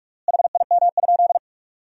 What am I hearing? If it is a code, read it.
Morse code, heard as a keyed tone, about 46 wpm, 690 Hz. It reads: HIM?
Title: 3x Word-Spacing - Sentences from Top 100 Words - 45wpm